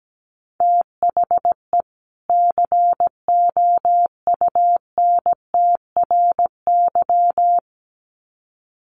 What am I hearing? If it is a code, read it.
Morse code, heard as a keyed tone, about 17 wpm, 700 Hz. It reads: THE COUNTRY
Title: Sets of 2 Words 17wpm